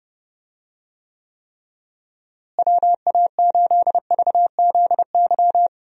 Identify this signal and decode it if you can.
Morse code, heard as a keyed tone, about 31 wpm, 700 Hz. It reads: WA8VZY
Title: USA Call Signs - 6 Characters - Repeated 3x - 30wpm